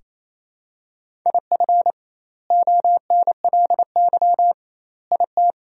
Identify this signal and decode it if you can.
Morse code, heard as a keyed tone, about 28 wpm, 700 Hz. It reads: IF ONLY IT
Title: 2x Word-Spacing - Sentences from Top 500 Words - 28wpm